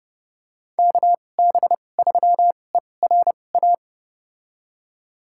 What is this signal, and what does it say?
Morse code, heard as a keyed tone, about 30 wpm, spaced as for 22 wpm, 700 Hz. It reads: KB3ERA
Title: USA Call Signs - 2x Character Spacing - 30wpm